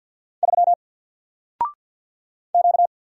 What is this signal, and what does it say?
Morse code, heard as a keyed tone, about 48 wpm, 700 Hz. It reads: 3 Q
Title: Single Letter-Number Alt 50wpm